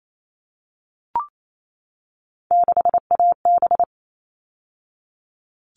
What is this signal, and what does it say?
Morse code, heard as a keyed tone, about 28 wpm, 700 Hz. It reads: 6AB
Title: All Cut Numbers - 3-Digit 28wpm